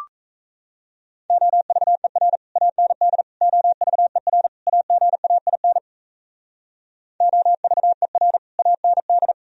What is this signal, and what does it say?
Morse code, heard as a keyed tone, about 34 wpm, 690 Hz. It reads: OVER AND OVER AGAIN OVER AND
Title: Speed-Racing - Sets of 4 Words 28wpm